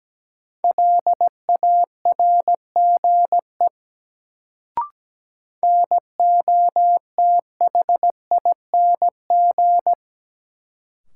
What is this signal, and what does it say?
Morse code, heard as a keyed tone, about 17 wpm, 700 Hz. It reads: LARGE NOTHING
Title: Sets of 4 Words - New Word Drill - 17wpm